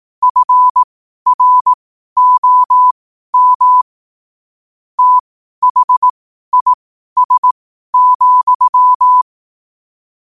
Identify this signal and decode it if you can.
Morse code, heard as a keyed tone, about 18 wpm, spaced as for 12 wpm, 1000 Hz. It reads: FROM THIS,
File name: qotdfiles / 12wpm